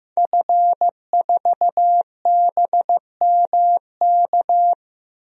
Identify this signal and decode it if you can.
Morse code, heard as a keyed tone, about 15 wpm, 700 Hz. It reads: F4BMK
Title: DXCC-100 Call Signs - 5 Characters - 15wpm